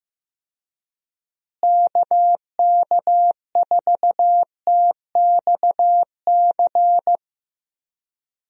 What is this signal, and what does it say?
Morse code, heard as a keyed tone, about 15 wpm, 700 Hz. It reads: KK4TXC
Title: USA Call Signs - 6 Characters - Repeated 3x - 15wpm